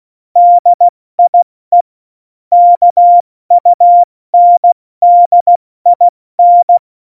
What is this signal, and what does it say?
Morse code, heard as a keyed tone, about 16 wpm, 700 Hz. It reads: DIE KUNDIN
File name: Morsekurs / Witze / Witz 20. November 2018, 70 F80